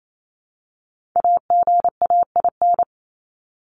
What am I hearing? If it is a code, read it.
Morse code, heard as a keyed tone, about 28 wpm, 700 Hz. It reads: AGAIN
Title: Top 200 Words 28wpm